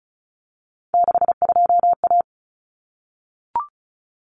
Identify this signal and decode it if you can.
Morse code, heard as a keyed tone, about 34 wpm, 700 Hz. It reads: 62A
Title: Common Cut Numbers - 3-Digit 35wpm